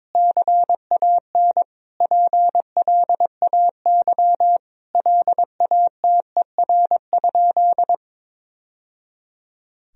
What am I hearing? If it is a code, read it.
Morse code, heard as a keyed tone, about 21 wpm, 700 Hz. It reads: CAN PLAY LATER?
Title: Sentences from Top 1000 Words - Review Part 4 - 22wpm